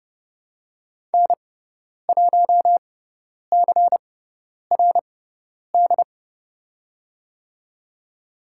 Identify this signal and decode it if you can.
Morse code, heard as a keyed tone, about 30 wpm, spaced as for 10 wpm, 700 Hz. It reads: N1CRD
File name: Lesson 16g - Letter c - Call Signs 30wpm